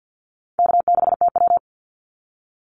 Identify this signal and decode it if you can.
Morse code, heard as a keyed tone, about 50 wpm, 700 Hz. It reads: K6TW
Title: NA SOTA Activator Callsigns 50wpm